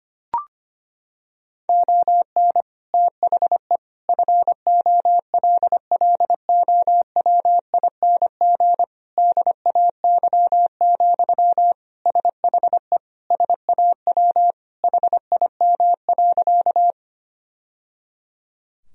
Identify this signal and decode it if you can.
Morse code, heard as a keyed tone, about 25 wpm, 700 Hz. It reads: ON THE FOLLOWING DAY, SHE SAW HIM.
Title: Sentences from Top 1000 Words - Review Part 4 - 25wpm